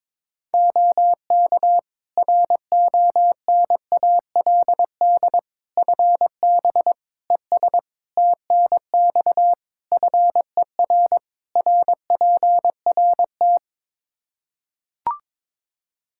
Morse code, heard as a keyed tone, about 22 wpm, 700 Hz. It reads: OK RONALD FB ES TNX FER RPRT
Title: Practice QSO - Exchange 2 - Info - 22wpm